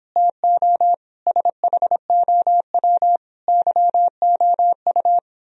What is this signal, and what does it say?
Morse code, heard as a keyed tone, about 26 wpm, 700 Hz. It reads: TO SHOW YOU